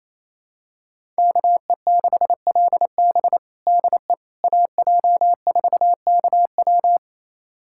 Morse code, heard as a keyed tone, about 28 wpm, 700 Hz. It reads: KE6LB DE AJ4KW
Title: USA Callsign DE Callsign - 5 Characters - 28wpm